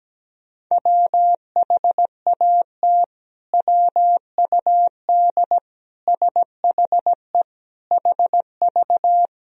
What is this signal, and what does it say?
Morse code, heard as a keyed tone, about 17 wpm, 700 Hz. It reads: WHAT WUD SHE HV
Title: Sentences from Top 100 Words and Abbreviations 17wpm